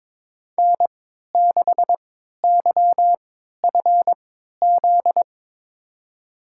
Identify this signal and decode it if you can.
Morse code, heard as a keyed tone, about 22 wpm, spaced as for 13 wpm, 700 Hz. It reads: N6YFZ
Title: USA Call Signs - 3x Character Spacing - 22wpm